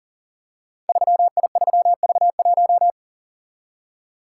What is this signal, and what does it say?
Morse code, heard as a keyed tone, about 39 wpm, 700 Hz. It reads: 3I3V1
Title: Code Groups - 5 Letter:Numbers - 40wpm